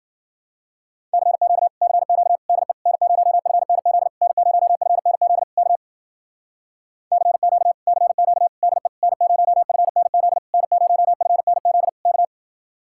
Morse code, heard as a keyed tone, about 50 wpm, 700 Hz. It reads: CQ CQ DE N0FNZ N0FNZ K CQ CQ DE N0FNZ N0FNZ K